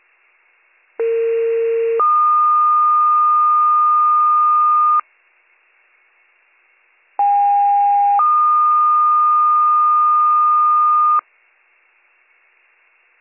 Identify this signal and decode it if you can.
Two-tone page, heard as a one-second tone, then a three-second tone, about 460 then 1220 Hz, about 810 then 1220 Hz. It is heard from 1.0 to 11.2 s.